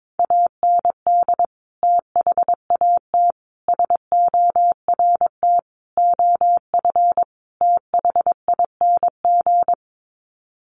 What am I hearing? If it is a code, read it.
Morse code, heard as a keyed tone, about 22 wpm, 700 Hz. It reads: AND THAT SORT OF THING
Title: Sets of 5 Words Encore2 22wpm